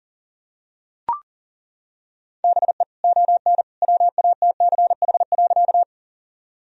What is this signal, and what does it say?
Morse code, heard as a keyed tone, about 37 wpm, 700 Hz. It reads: BE ON WATCH.
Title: Sentences from Top 1000 Words - Review Part 1 - 40wpm